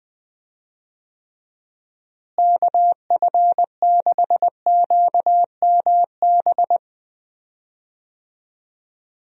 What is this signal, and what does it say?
Morse code, heard as a keyed tone, about 20 wpm, 700 Hz. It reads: KF6QMB
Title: USA Call Signs - 6 Characters - Repeated 3x - 20wpm